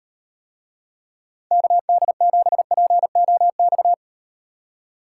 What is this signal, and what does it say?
Morse code, heard as a keyed tone, about 38 wpm, 700 Hz. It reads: KD7POX